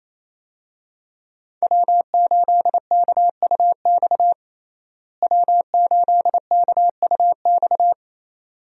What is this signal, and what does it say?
Morse code, heard as a keyed tone, about 27 wpm, 700 Hz. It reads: W8KUX W8KUX
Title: USA Call Signs - 5 Characters - Repeated 2x - 28wpm